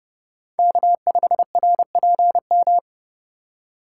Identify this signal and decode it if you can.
Morse code, heard as a keyed tone, about 31 wpm, 700 Hz. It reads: K5RPM
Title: USA Call Signs - 8x Character Spacing - 30wpm